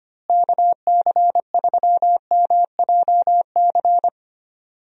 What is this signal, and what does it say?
Morse code, heard as a keyed tone, about 25 wpm, 700 Hz. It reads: KC3MJC
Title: USA Call Signs - 6 Characters - Repeated 3x - 25wpm